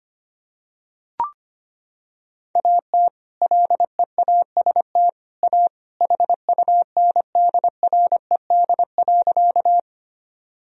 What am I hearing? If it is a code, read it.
Morse code, heard as a keyed tone, about 25 wpm, 700 Hz. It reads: AT LEAST A HUNDRED.
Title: Sentences from Top 1000 Words - Review Part 2 - 25wpm